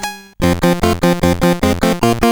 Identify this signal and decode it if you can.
There is no code.